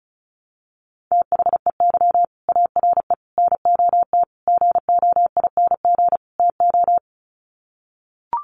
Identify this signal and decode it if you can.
Morse code, heard as a keyed tone, about 34 wpm, 700 Hz. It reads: THEY ARE NOT GOING TO
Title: Sets of 5 Words Encore2 35wpm